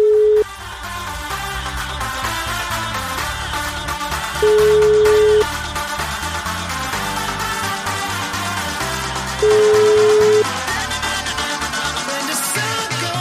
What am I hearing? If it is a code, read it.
Ringback tone (Europe, 425 Hz), with noise.